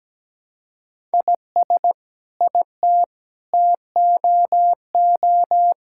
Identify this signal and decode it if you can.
Morse code, heard as a keyed tone, about 17 wpm, 700 Hz. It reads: IS IT TOO
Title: Sentences from Top 1000 Words - Review Part 6 - 17wpm